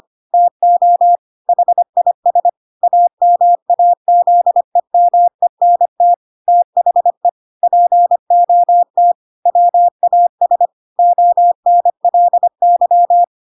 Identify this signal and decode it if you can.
Morse code, heard as a keyed tone, about 25 wpm, 700 Hz. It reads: TO HIS AMAZEMENT THE POT WAS ONLY
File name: Pinocchio / Pinocchio ch5 25 25 ik1hge